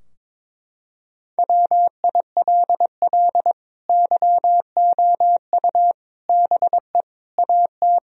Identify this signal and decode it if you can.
Morse code, heard as a keyed tone, about 22 wpm, 690 Hz. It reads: WILL YOU BE AT
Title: Sentences from Top 1000 Words - Review Part 3 - 22wpm